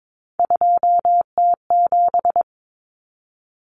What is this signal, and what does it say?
Morse code, heard as a keyed tone, about 22 wpm, 700 Hz. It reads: 2T7
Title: Common Cut Numbers - 3-Digit 22wpm